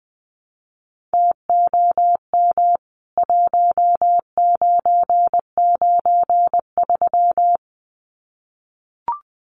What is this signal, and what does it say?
Morse code, heard as a keyed tone, about 20 wpm, 700 Hz. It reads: TOM 1993
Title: CWT Exchange 20wpm